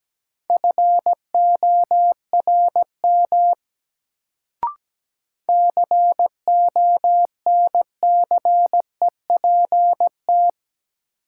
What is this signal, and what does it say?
Morse code, heard as a keyed tone, about 17 wpm, 700 Hz. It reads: FORM CONCEPT
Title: Sentences from Top 1100 Words - New Word Drill Alt - 17wpm